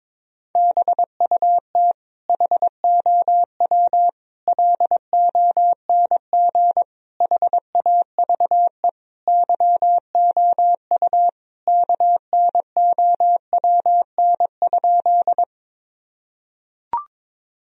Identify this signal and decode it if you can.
Morse code, heard as a keyed tone, about 22 wpm, 700 Hz. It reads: BUT HOW LONG HAVE YOU KNOWN?
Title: Sentences from Top 1000 Words - Review Part 3 - 22wpm